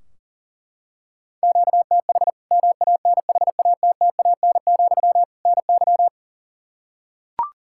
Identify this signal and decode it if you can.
Morse code, heard as a keyed tone, about 40 wpm, 700 Hz. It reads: QTH MANHATTAN, NY
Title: Practice QSO - Exchange 1 - Beginning - 40wpm